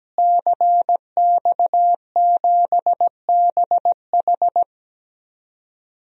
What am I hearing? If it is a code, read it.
Morse code, heard as a keyed tone, about 17 wpm, 700 Hz. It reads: CX7BH
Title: DXCC-100 Call Signs - 5 Characters - Repeated 2x - 17wpm